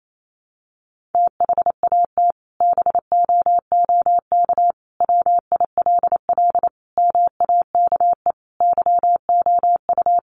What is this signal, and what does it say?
Morse code, heard as a keyed tone, about 28 wpm, 700 Hz. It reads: THAT BOOK WILL MAKE YOU
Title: Sentences from Top 1500 Words 28wpm